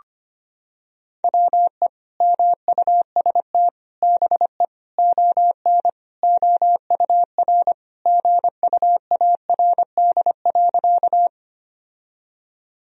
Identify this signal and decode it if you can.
Morse code, heard as a keyed tone, about 25 wpm, 700 Hz. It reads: WE MUST BE ON OUR GUARD.
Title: Sentences from Top 1000 Words - Review Part 4 - 25wpm